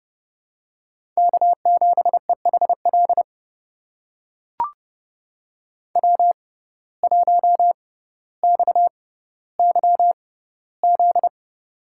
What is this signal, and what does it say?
Morse code, heard as a keyed tone, about 30 wpm, spaced as for 15 wpm, 700 Hz. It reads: K7EHL W1XYZ
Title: USA Call Signs - 5 Characters - 6x Character Spacing - 30wpm